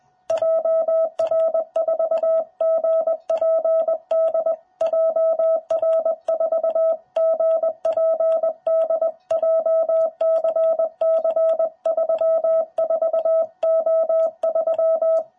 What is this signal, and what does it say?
Morse code, heard as a keyed tone, about 20 wpm, 660 Hz. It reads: JR4GPD JR4GPD JCC34O3
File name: jh8cbh / 2021cw